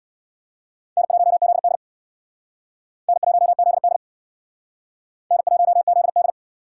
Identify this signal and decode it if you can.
Morse code, heard as a keyed tone, about 60 wpm, 700 Hz. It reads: N1ZD N1ZD N1ZD
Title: Speed Racing - USA 1x2 Call Signs 50wpm